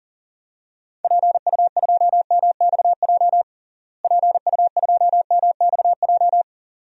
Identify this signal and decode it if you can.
Morse code, heard as a keyed tone, about 40 wpm, 700 Hz. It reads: PU2MXJ PU2MXJ
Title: DXCC-100 Call Signs - 6 Characters - Repeated 2x - 40wpm